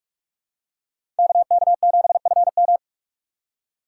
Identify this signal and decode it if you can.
Morse code, heard as a keyed tone, about 46 wpm, 700 Hz. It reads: KK7FM
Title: USA Call Signs - 5 Characters - 2x Character Spacing - 45wpm